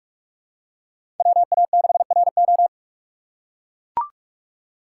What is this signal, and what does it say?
Morse code, heard as a keyed tone, about 42 wpm, 700 Hz. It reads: WA6RO
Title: USA Call Signs - 5 Characters - 6x Character Spacing - 45wpm